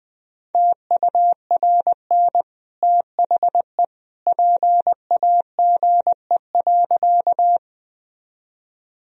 Morse code, heard as a keyed tone, about 20 wpm, 700 Hz. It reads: TURN THE PAGE.
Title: Sentences from Top 1000 Words - Review Part 1 - 20wpm